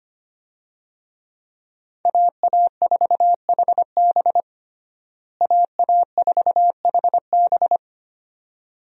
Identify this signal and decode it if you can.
Morse code, heard as a keyed tone, about 25 wpm, 700 Hz. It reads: AA4HB AA4HB
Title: USA Call Signs - 5 Characters - Repeated 2x - 25wpm